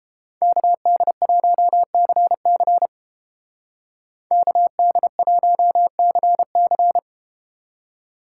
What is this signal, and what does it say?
Morse code, heard as a keyed tone, about 32 wpm, 700 Hz. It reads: KD1CC KD1CC